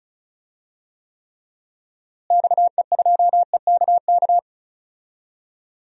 Morse code, heard as a keyed tone, about 34 wpm, 700 Hz. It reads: XE2EKK
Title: DXCC-100 Call Signs - 6 Characters - Repeated 2x - 35wpm